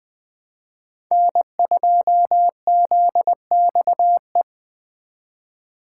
Morse code, heard as a keyed tone, about 24 wpm, 700 Hz. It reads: N2ZXE
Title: Speed Racing - USA 1x3 Call Signs 20wpm